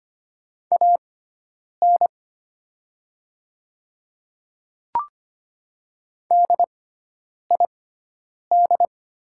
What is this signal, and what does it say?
Morse code, heard as a keyed tone, about 25 wpm, spaced as for 9 wpm, 700 Hz. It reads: AN DID
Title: 6x Character Spacing - Top 500 Words - 25wpm